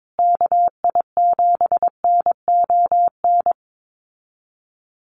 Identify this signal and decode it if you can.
Morse code, heard as a keyed tone, about 22 wpm, 700 Hz. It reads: KI7NON
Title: USA Call Signs Repeated 3x 22wpm